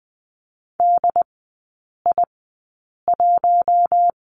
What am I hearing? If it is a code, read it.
Morse code, heard as a keyed tone, about 20 wpm, spaced as for 8 wpm, 700 Hz. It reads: DI1
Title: 067 Lesson 12e - Letter d - Three Characters 20wpm